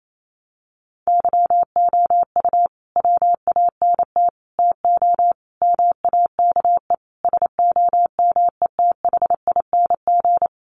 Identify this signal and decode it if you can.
Morse code, heard as a keyed tone, about 28 wpm, 700 Hz. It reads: YOU WANT TO MAKE SOMETHING
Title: Sentences from Top 600 Words 28wpm